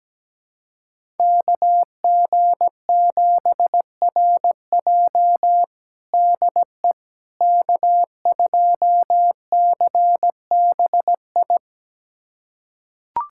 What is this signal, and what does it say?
Morse code, heard as a keyed tone, about 17 wpm, 700 Hz. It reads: KG7RJ DE K2CBI
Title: USA Callsign DE Callsign - 5 Characters - 17wpm